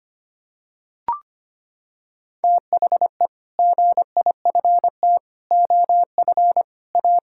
Morse code, heard as a keyed tone, about 25 wpm, 700 Hz. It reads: THE GIFT OF A
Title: Sentences from Top 1000 Words - Review Part 6 - 25wpm